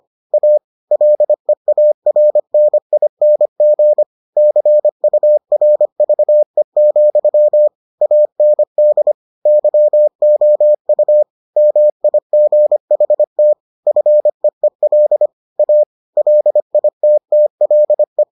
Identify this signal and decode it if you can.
Morse code, heard as a keyed tone, about 25 wpm, 600 Hz. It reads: A LEARNING CURVE, AND YOU MIGHT FEEL A LITTLE